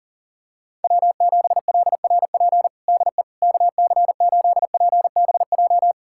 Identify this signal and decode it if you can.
Morse code, heard as a keyed tone, about 40 wpm, 700 Hz. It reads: W7LRP DE KC8PBJ
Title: USA Callsign DE Callsign 40wpm